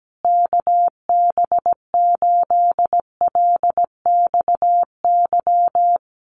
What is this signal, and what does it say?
Morse code, heard as a keyed tone, about 17 wpm, 700 Hz. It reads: KB8LXY
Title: Common QSO Phrases Encore 17wpm